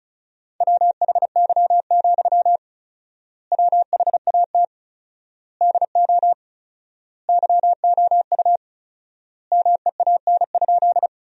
Morse code, heard as a keyed tone, about 34 wpm, 700 Hz. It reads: WHY, WHAT DO YOU MEAN?
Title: 4x Word-Spacing - Sentences from Top 200 Words - 35wpm